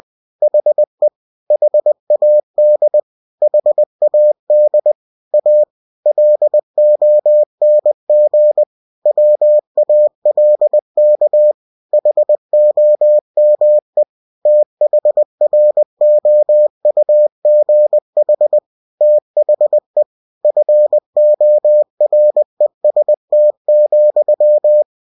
Morse code, heard as a keyed tone, about 20 wpm, 600 Hz. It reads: HE HAD HAD A LONG WALK HOME THROUGH THE FOREST,
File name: cw-books / winnie-the-pooh / winnie-20wpm-page-0070